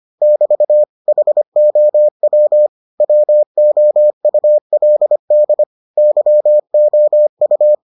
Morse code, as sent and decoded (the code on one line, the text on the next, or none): -...- / .... --- .-- / .-- --- ..- .-.. -.. / -.-- --- ..-
<BT> HOW WOULD YOU